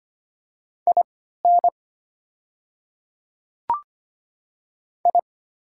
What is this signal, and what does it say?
Morse code, heard as a keyed tone, about 25 wpm, spaced as for 14 wpm, 690 Hz. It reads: IN I
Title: 3x Character Spacing - Top 100 Words - 25wpm